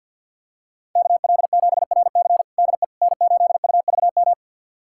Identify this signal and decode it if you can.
Morse code, heard as a keyed tone, about 51 wpm, 700 Hz. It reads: KL7RC DE N8UVM
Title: USA Callsign DE Callsign - 5 Characters - 50wpm